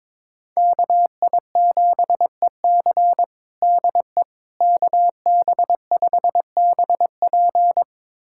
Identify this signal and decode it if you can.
Morse code, heard as a keyed tone, about 22 wpm, 700 Hz. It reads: KI7EC DE KB5BP
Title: USA Callsign DE Callsign - 5 Characters - 22wpm